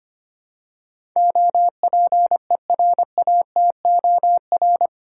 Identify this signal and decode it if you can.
Morse code, heard as a keyed tone, about 25 wpm, 700 Hz. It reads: OPERATOR
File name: Eight Letter Words - 25wpm